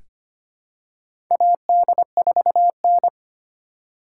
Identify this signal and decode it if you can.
Morse code, heard as a keyed tone, about 25 wpm, 700 Hz. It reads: AD4N
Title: USA Call Signs - 4 Characters - 2x Character Spacing - 25wpm